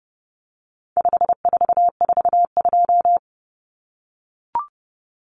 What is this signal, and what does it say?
Morse code, heard as a keyed tone, about 30 wpm, 700 Hz. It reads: 5442
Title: Numbers - 4-Digit 30wpm